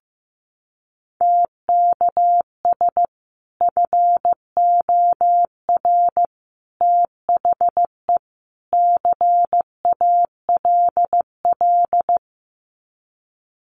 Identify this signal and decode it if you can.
Morse code, heard as a keyed tone, about 15 wpm, 700 Hz. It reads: TKS FOR THE CALL
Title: Common QSO Phrases 15wpm